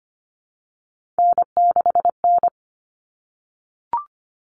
Morse code, heard as a keyed tone, about 25 wpm, 700 Hz. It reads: N6N
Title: All Cut Numbers - 3-Digit 25wpm